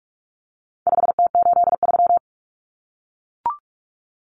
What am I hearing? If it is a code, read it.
Morse code, heard as a keyed tone, about 44 wpm, 700 Hz. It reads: 5T83
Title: Common Cut Numbers - 4-Digit 45wpm